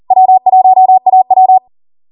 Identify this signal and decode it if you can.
Morse code, heard as a keyed tone, about 40 wpm, 750 Hz. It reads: W1AW